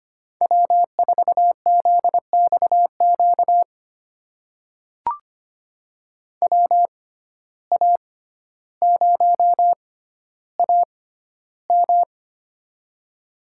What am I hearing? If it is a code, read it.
Morse code, heard as a keyed tone, about 25 wpm, spaced as for 13 wpm, 700 Hz. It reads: W4ZXQ WA0AM
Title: USA Call Signs - 5 Characters - 6x Character Spacing - 25wpm